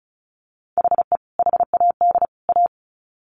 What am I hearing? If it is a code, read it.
Morse code, heard as a keyed tone, about 34 wpm, 700 Hz. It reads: HE HAD A
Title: Sets of 3 Words 35wpm